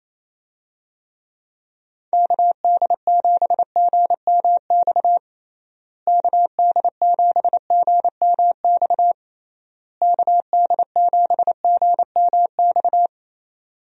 Morse code, heard as a keyed tone, about 28 wpm, 700 Hz. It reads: KD7GMX KD7GMX KD7GMX
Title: USA Call Signs - 6 Characters - Repeated 3x - 28wpm